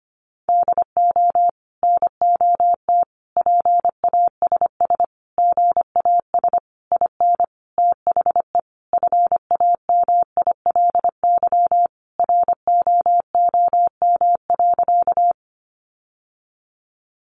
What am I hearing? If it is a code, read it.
Morse code, heard as a keyed tone, about 25 wpm, 700 Hz. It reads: DO NOT PASS GAS IN THE FAMILY ROOM.
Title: Sentences from Top 1100 Words 25wpm